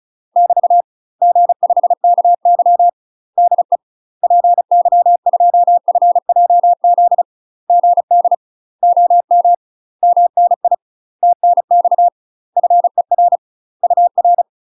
Morse code, heard as a keyed tone, about 35 wpm, 700 Hz. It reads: <BT> G5KY DE PY2FJZ GD OM MNI TNX FER UR
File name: UK MASTER QSOs TRAINING/35WPM